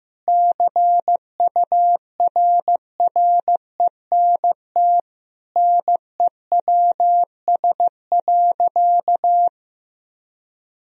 Morse code, heard as a keyed tone, about 15 wpm, 700 Hz. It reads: CURRENT NEWS.
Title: Sentences from Top 1000 Words - Review Part 4 - 15wpm